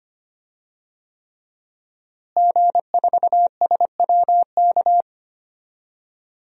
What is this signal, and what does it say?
Morse code, heard as a keyed tone, about 25 wpm, 700 Hz. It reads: G4SWK
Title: DXCC-100 Call Signs - 5 Characters - Repeated 3x - 25wpm